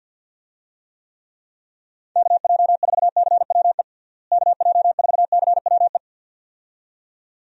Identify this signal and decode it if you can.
Morse code, heard as a keyed tone, about 49 wpm, 700 Hz. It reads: KJ4CWE KJ4CWE
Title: USA Call Signs - 6 Characters - Repeated 2x - 50wpm